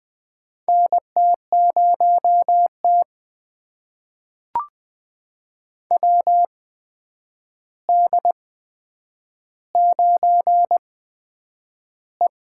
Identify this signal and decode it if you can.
Morse code, heard as a keyed tone, about 20 wpm, spaced as for 9 wpm, 700 Hz. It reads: NT0T WD9E
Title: USA Call Signs - 8x Character Spacing - 20wpm